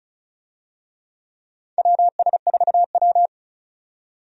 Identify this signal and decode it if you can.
Morse code, heard as a keyed tone, about 34 wpm, 700 Hz. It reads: WS4W
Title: USA Call Signs - 4 Characters - 35wpm